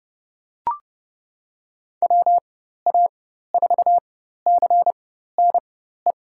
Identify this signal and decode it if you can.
Morse code, heard as a keyed tone, about 32 wpm, spaced as for 14 wpm, 700 Hz. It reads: WA4CNE
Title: USA Call Signs - 4x Character Spacing - 30wpm